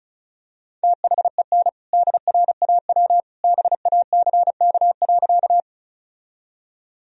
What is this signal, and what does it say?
Morse code, heard as a keyed tone, about 34 wpm, 700 Hz. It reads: THEN DRAW BACK.
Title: Sentences from Top 1000 Words - Review Part 1 - 35wpm